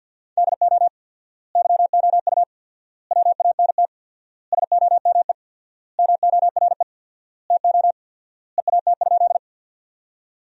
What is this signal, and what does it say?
Morse code, heard as a keyed tone, about 50 wpm, 700 Hz. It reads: DO YOU WANT SOME MORE TO EAT?
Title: 4x Word-Spacing - Sentences from Top 300 Words - 50wpm